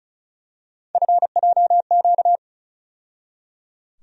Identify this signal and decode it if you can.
Morse code, heard as a keyed tone, about 34 wpm, 700 Hz. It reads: FJQ
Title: USA Call Sign Postfixes - 3 Letters 35wpm